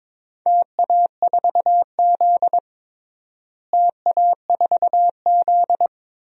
Morse code, heard as a keyed tone, about 22 wpm, 700 Hz. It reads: TA4Z TA4Z
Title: DXCC-100 Call Signs - 4 Characters - Repeated 2x - 22wpm